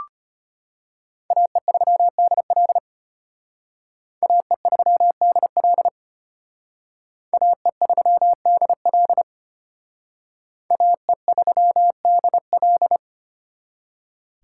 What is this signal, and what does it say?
Morse code, heard as a keyed tone, about 32 wpm, 700 Hz. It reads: AE3DL AE3DL AE3DL AE3DL